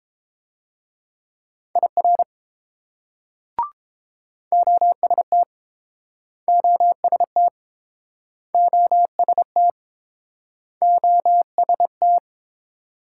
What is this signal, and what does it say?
Morse code, heard as a keyed tone, about 30 wpm, 700 Hz. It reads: IR OST OST OST OST